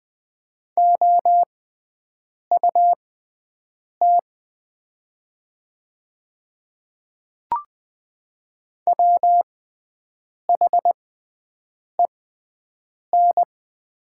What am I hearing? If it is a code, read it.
Morse code, heard as a keyed tone, about 20 wpm, spaced as for 7 wpm, 700 Hz. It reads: OUT WHEN
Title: 6x Character Spacing - Top 200 Words - 20wpm